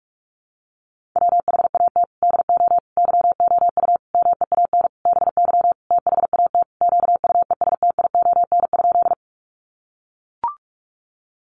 Morse code, heard as a keyed tone, about 44 wpm, 700 Hz. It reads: WHAT DO YOU MEAN BY THAT QUESTION?
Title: Sentences from Top 400 Words 45wpm